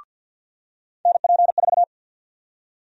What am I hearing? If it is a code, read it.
Morse code, heard as a keyed tone, about 52 wpm, 700 Hz. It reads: NP4
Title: USA Call Sign Prefixes - 50wpm